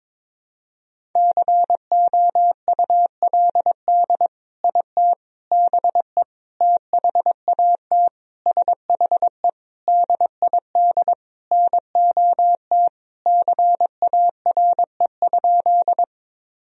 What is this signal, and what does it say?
Morse code, heard as a keyed tone, about 22 wpm, 700 Hz. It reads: COULD IT BE THAT SHE DID NOT CARE?
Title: Sentences from Top 1000 Words - Review Part 1 - 22wpm